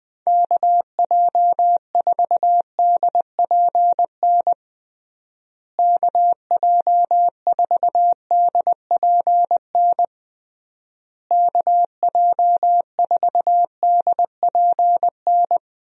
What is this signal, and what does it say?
Morse code, heard as a keyed tone, about 20 wpm, 700 Hz. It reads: KJ4DPN KJ4DPN KJ4DPN
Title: USA Call Signs - 6 Characters - Repeated 3x - 20wpm